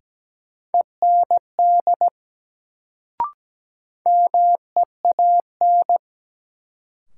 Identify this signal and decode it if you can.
Morse code, heard as a keyed tone, about 17 wpm, 700 Hz. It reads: END MEAN
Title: Sentences from Top 200 Words - New Word Drill - 17wpm